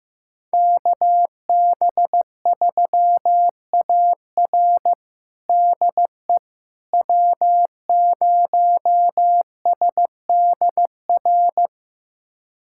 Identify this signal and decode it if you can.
Morse code, heard as a keyed tone, about 15 wpm, 700 Hz. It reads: KB3AR DE W0SDR
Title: USA Callsign DE Callsign - 5 Characters - 15wpm